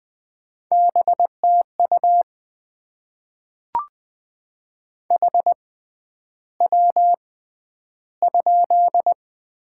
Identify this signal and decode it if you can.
Morse code, heard as a keyed tone, about 20 wpm, spaced as for 10 wpm, 700 Hz. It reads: BTU HW?
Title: Top 20 QSO Elements - 6x Character Spacing - 20wpm